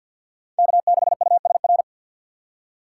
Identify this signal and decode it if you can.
Morse code, heard as a keyed tone, about 60 wpm, 700 Hz. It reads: K6USR
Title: Speed Racing - USA 1x3 Call Signs 50wpm